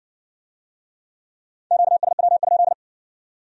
Morse code, heard as a keyed tone, about 50 wpm, 700 Hz. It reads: QSW?
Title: Speed-Racing Q-Codes - 40wpm